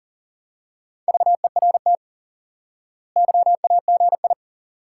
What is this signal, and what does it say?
Morse code, heard as a keyed tone, about 37 wpm, 700 Hz. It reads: VERT YAGI
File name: IQE Course 3 - Lesson 89 - YAGI - Two Elements - 40wpm